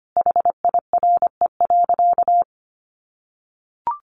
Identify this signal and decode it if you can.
Morse code, heard as a keyed tone, about 25 wpm, 700 Hz. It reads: HIRE.
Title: Sentences from Top 1400 Words 25wpm